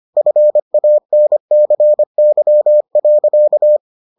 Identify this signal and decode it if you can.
Morse code, heard as a keyed tone, about 25 wpm, 600 Hz. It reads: FANCY.